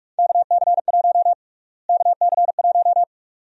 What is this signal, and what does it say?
Morse code, heard as a keyed tone, about 46 wpm, 700 Hz. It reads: KC1 KC1
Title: USA Call Sign Prefixes - Repeated 2x - 45wpm